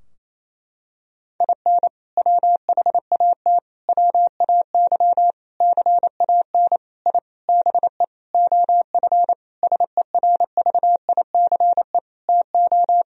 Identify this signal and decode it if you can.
Morse code, heard as a keyed tone, about 28 wpm, 700 Hz. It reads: IN WHAT WAY CAN I BE OF SERVICE TO
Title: Sentences from Top 1000 Words - Review Part 4 - 28wpm